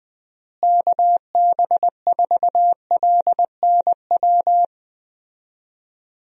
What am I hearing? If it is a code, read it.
Morse code, heard as a keyed tone, about 20 wpm, 700 Hz. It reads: KB4LNW
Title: USA Call Signs - 6 Characters - Repeated 2x - 20wpm